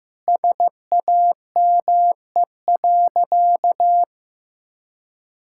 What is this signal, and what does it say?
Morse code, heard as a keyed tone, about 15 wpm, 700 Hz. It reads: SAME.
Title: Sentences from Top 1600 Words 15wpm